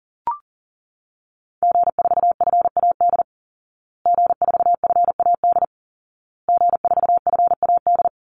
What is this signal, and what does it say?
Morse code, heard as a keyed tone, about 40 wpm, 700 Hz. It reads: G4FAD G4FAD G4FAD
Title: Call Signs Repeated 2x - 40wpm